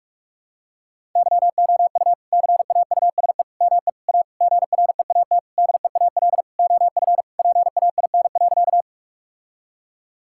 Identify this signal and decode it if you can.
Morse code, heard as a keyed tone, about 47 wpm, 700 Hz. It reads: YOU CAUSE ME A GREAT DEAL OF PAIN.
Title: Sentences from Top 1600 Words 45wpm